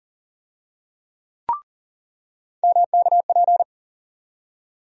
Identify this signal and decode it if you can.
Morse code, heard as a keyed tone, about 40 wpm, 700 Hz. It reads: MKP
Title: USA Call Sign Postfixes - 3 Letters 40wpm